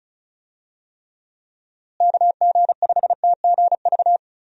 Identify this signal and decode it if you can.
Morse code, heard as a keyed tone, about 36 wpm, 700 Hz. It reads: KG5TGV
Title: USA Call Signs - 6 Characters - Repeated 3x - 35wpm